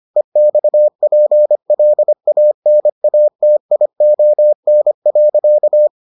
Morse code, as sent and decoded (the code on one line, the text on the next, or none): . -..- .--. .-.. .- -. .- - .. --- -. .-.-.-
EXPLANATION.